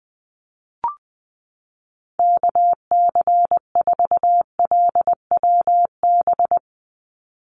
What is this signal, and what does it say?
Morse code, heard as a keyed tone, about 20 wpm, 700 Hz. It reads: KC4LWB
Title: USA Call Signs 20wpm